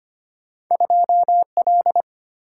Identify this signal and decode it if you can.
Morse code, heard as a keyed tone, about 25 wpm, 700 Hz. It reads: 2L
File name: Two Letter-Number Combinations - 25wpm